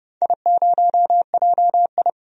Morse code, heard as a keyed tone, about 31 wpm, 700 Hz. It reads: I0JI
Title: DXCC-100 Call Signs - 4 Characters - 30wpm